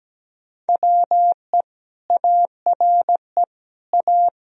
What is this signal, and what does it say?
Morse code, heard as a keyed tone, about 17 wpm, 700 Hz. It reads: WE ARE A
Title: Sentences from Top 1000 Words - Review Part 4 - 17wpm